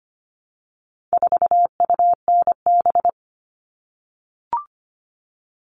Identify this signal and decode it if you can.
Morse code, heard as a keyed tone, about 25 wpm, 700 Hz. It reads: 4UNB
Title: All Cut Numbers - 4-Digit 25wpm